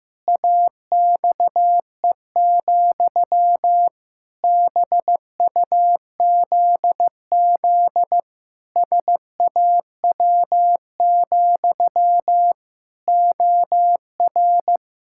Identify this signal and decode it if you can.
Morse code, heard as a keyed tone, about 15 wpm, 700 Hz. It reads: AXE, BUZZ SAW, OR
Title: Pangram Challenge v2 - 15wpm